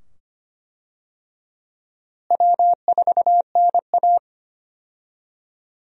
Morse code, heard as a keyed tone, about 25 wpm, 700 Hz. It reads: W4NA
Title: USA Call Signs - 4 Characters - Repeated 2x - 25wpm